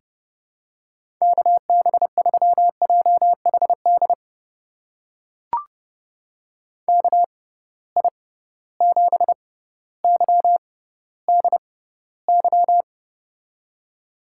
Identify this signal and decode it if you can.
Morse code, heard as a keyed tone, about 30 wpm, spaced as for 15 wpm, 700 Hz. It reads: KB3JHD KI7YDY
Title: USA Call Signs - 6 Characters - 6x Character Spacing - 30wpm